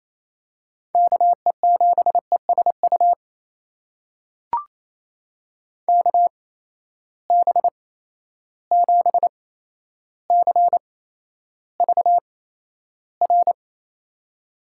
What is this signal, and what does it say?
Morse code, heard as a keyed tone, about 28 wpm, spaced as for 12 wpm, 700 Hz. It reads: KE7ESU KB7CVR
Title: USA Call Signs - 6 Characters - 8x Character Spacing - 28wpm